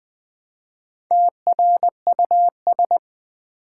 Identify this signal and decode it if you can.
Morse code, heard as a keyed tone, about 20 wpm, 700 Hz. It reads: TRUS
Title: Build A Word - 1 Letter At A Time - 5-Letter Words - 20wpm